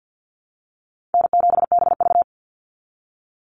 Morse code, heard as a keyed tone, about 49 wpm, 700 Hz. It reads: N7BV
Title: Easy Call Signs Repeated 2x - 50wpm